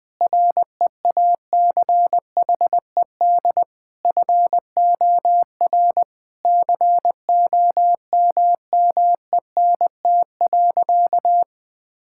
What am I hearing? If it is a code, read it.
Morse code, heard as a keyed tone, about 20 wpm, 700 Hz. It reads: REACHED FOR COMMENT.
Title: Sentences with 7-Word Phrases - 20wpm